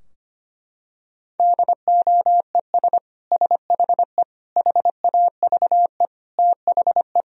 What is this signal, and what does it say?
Morse code, heard as a keyed tone, about 25 wpm, 700 Hz. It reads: DOES SHE HAVE THE
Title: Sentences from Top 1000 Words - Review Part 3 - 25wpm